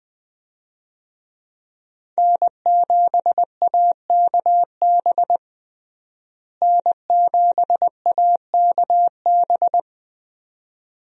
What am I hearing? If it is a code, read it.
Morse code, heard as a keyed tone, about 20 wpm, 700 Hz. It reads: N7AKB N7AKB
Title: USA Call Signs - 5 Characters - Repeated 2x - 20wpm